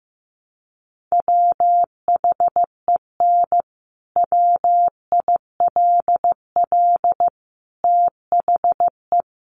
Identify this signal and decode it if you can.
Morse code, heard as a keyed tone, about 15 wpm, 700 Hz. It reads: WHEN WILL THE
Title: Sentences from Top 1300 Words 15wpm